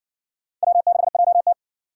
Morse code, heard as a keyed tone, about 57 wpm, 700 Hz. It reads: W6JT